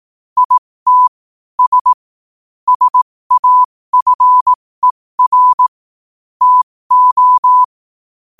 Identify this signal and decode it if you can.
Morse code, heard as a keyed tone, about 18 wpm, spaced as for 15 wpm, 1000 Hz. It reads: IT S SAFER TO
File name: qotdfiles / 15wpm